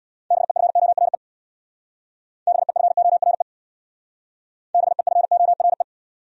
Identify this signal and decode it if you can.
Morse code, heard as a keyed tone, about 65 wpm, 700 Hz. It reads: BEFORE BEFORE BEFORE